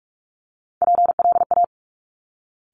Morse code, heard as a keyed tone, about 46 wpm, 690 Hz. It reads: PLA
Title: Top Three Letter Combinations in Words 45wpm